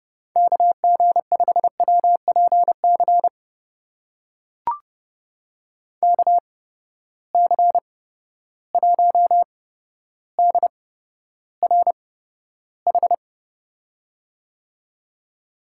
Morse code, heard as a keyed tone, about 30 wpm, spaced as for 13 wpm, 700 Hz. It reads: KG5WPC KC1DRH
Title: USA Call Signs - 6 Characters - 8x Character Spacing - 30wpm